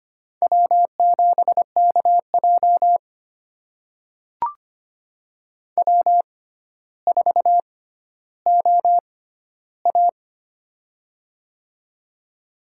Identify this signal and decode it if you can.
Morse code, heard as a keyed tone, about 25 wpm, spaced as for 13 wpm, 700 Hz. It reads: W7KJ W4OA
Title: USA Call Signs - 4 Characters - 6x Character Spacing - 25wpm